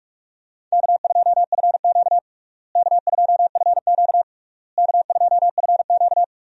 Morse code, heard as a keyed tone, about 45 wpm, 700 Hz. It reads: K2FQ K2FQ K2FQ
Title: USA Call Signs - 4 Characters - Repeated 3x - 45wpm